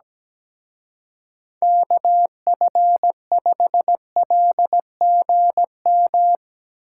Morse code, heard as a keyed tone, about 18 wpm, 700 Hz. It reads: KF5LGM
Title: Speed Racing - USA 2x3 Call Signs 17wpm